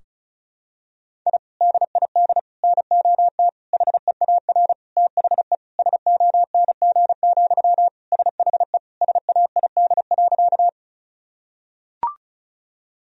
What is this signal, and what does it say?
Morse code, heard as a keyed tone, about 36 wpm, 700 Hz. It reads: I DID NOT HEAR THE SONG, SHE SAID.
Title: Sentences from Top 1000 Words - Review Part 2 - 35wpm